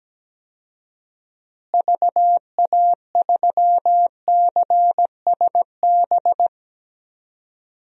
Morse code, heard as a keyed tone, about 17 wpm, 700 Hz. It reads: VA3CSB
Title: DXCC-100 Call Signs - 6 Characters - Repeated 3x - 17wpm